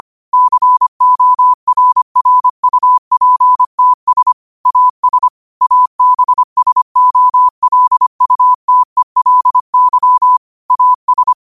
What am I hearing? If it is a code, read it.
Morse code, heard as a keyed tone, about 25 wpm, 1000 Hz. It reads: CORRUPTS AS ABSOLUTELY AS
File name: qotdfiles / 25wpm